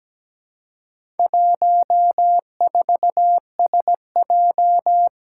Morse code, heard as a keyed tone, about 17 wpm, 700 Hz. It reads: 14SJ
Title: Four Challenging Letter-Number Combinations 17wpm